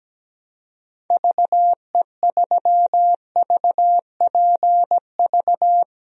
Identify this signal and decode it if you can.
Morse code, heard as a keyed tone, about 17 wpm, 700 Hz. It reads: VE3VPV
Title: DXCC-100 Call Signs - 6 Characters - Repeated 3x - 17wpm